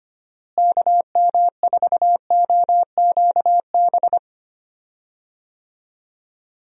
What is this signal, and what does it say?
Morse code, heard as a keyed tone, about 25 wpm, 700 Hz. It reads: KM4OQB
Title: USA Call Signs - 6 Characters - Repeated 2x - 25wpm